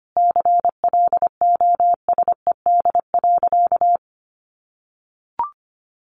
Morse code, heard as a keyed tone, about 25 wpm, 700 Hz. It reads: CLOSED.
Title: Sentences from Top 1400 Words 25wpm